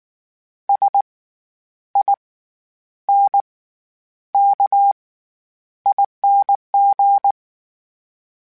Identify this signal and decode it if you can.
Morse code, heard as a keyed tone, about 19 wpm, spaced as for 9 wpm, 800 Hz. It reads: SINKING
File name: Special INGs / Special INGs-mp3s / SlowToFaster ing @18